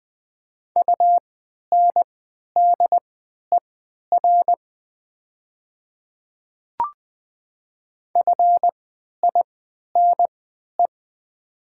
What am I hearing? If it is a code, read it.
Morse code, heard as a keyed tone, about 20 wpm, spaced as for 11 wpm, 700 Hz. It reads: UNDER FINE
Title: 3x Character Spacing - Top 500 Words - 20wpm